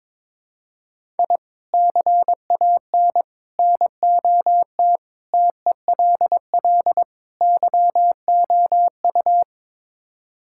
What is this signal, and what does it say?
Morse code, heard as a keyed tone, about 22 wpm, 700 Hz. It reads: I CAN NOT TELL YOU
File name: practice / 22wpm / Build A Phrase - 5-Word Phrases - 22wpm